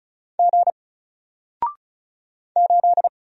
Morse code, heard as a keyed tone, about 35 wpm, 700 Hz. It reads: G 8
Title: Single Letter-Number 35wpm